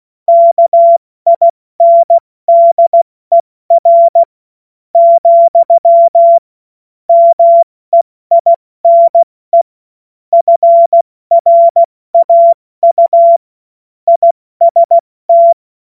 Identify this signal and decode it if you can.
Morse code, heard as a keyed tone, about 16 wpm, 680 Hz. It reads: KINDER , MEINE FRAU IST
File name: Morsekurs / Witze / Witz 03. August 2021, 70 F80